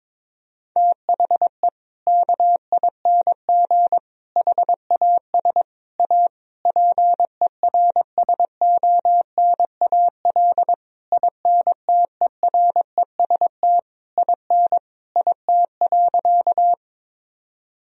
Morse code, heard as a keyed tone, about 22 wpm, 700 Hz. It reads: THE KING HAS A PERSONAL INTEREST IN IT.
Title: Sentences from Top 1000 Words - Review Part 4 - 22wpm